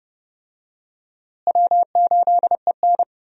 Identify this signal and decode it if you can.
Morse code, heard as a keyed tone, about 30 wpm, 700 Hz. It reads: W8EN
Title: USA Call Signs - 4 Characters - Repeated 3x - 30wpm